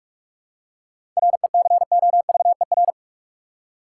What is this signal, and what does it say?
Morse code, heard as a keyed tone, about 46 wpm, 700 Hz. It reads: RECOVER
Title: Seven Letter Words - 45wpm